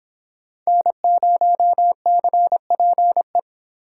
Morse code, heard as a keyed tone, about 24 wpm, 700 Hz. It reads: N0CPE